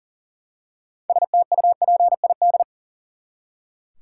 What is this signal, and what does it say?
Morse code, heard as a keyed tone, about 40 wpm, 700 Hz. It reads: STUPID
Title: Six Letter Words - 40wpm